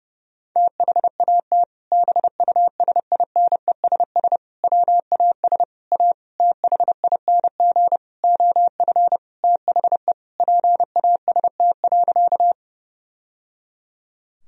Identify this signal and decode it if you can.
Morse code, heard as a keyed tone, about 31 wpm, 700 Hz. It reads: THAT BUSINESS WAS A THING OF THE PAST.
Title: Sentences from Top 1000 Words - Review Part 3 - 30wpm